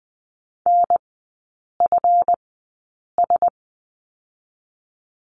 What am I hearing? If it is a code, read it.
Morse code, heard as a keyed tone, about 20 wpm, spaced as for 8 wpm, 700 Hz. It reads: NFS
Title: 152 Lesson 25e - Letter p - Three Characters 20wpm